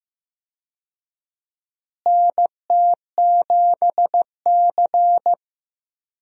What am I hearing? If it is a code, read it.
Morse code, heard as a keyed tone, about 15 wpm, 700 Hz. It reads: NT7C
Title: USA Call Signs - 4 Characters - Repeated 3x - 15wpm